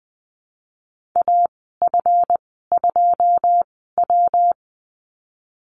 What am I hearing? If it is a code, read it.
Morse code, heard as a keyed tone, about 20 wpm, spaced as for 15 wpm, 700 Hz. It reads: AF2W
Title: 161 Lesson 26g - Letter g - Call Signs 20wpm